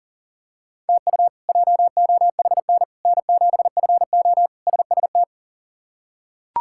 Morse code, heard as a keyed tone, about 40 wpm, 700 Hz. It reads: TU JOHN N7FO SST
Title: K1USN SST QSO Practice - 40wpm